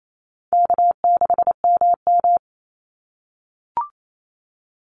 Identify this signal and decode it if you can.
Morse code, heard as a keyed tone, about 28 wpm, 700 Hz. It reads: K6MM
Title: Easy Call Signs - 28wpm